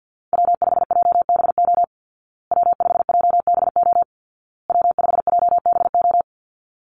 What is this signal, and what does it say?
Morse code, heard as a keyed tone, about 50 wpm, 700 Hz. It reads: W5JBO W5JBO W5JBO
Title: Call Signs Repeated 2x - 50wpm